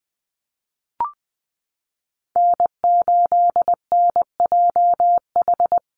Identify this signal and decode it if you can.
Morse code, heard as a keyed tone, about 20 wpm, 700 Hz. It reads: N8NJH
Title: USA Call Signs Repeated 3x 20wpm